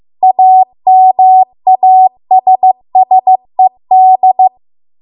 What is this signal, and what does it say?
Morse code, heard as a keyed tone, about 15 wpm, 750 Hz. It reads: AMASSED